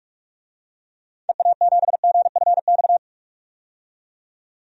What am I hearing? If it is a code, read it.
Morse code, heard as a keyed tone, about 46 wpm, 700 Hz. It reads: EA7GFX
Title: DXCC-100 Call Signs - 6 Characters - 45wpm